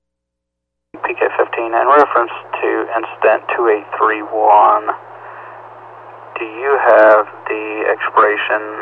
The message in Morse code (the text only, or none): none